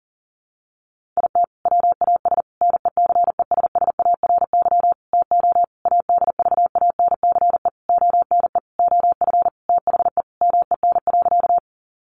Morse code, heard as a keyed tone, about 40 wpm, 700 Hz. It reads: IT WAS NECESSARY TO ADVANCE ONE OF THE MEN.
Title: Sentences from Top 1400 Words 40wpm